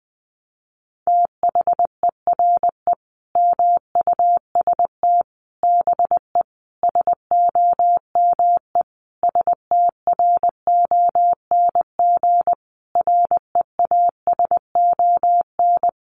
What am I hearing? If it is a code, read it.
Morse code, heard as a keyed tone, about 20 wpm, 700 Hz. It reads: THERE MUST BE SOME STRONG REASON
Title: Sentences from Top 700 Words 20wpm